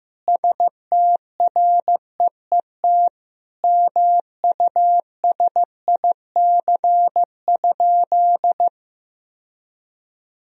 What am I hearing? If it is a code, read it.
Morse code, heard as a keyed tone, about 16 wpm, 700 Hz. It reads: STREET MUSIC?
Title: Sentences from Top 1000 Words - Review Part 3 - 15wpm